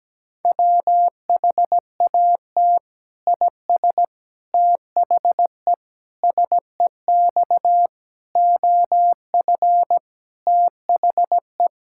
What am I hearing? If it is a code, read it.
Morse code, heard as a keyed tone, about 17 wpm, 700 Hz. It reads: WHAT IS THE SEX OF THE
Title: Sentences from Top 1600 Words 17wpm